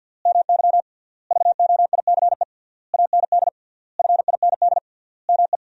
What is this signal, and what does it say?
Morse code, heard as a keyed tone, about 46 wpm, 700 Hz. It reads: MY VOICE AND FIND ME
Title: 3x Word-Spacing - Sentences from Top 500 Words - 50wpm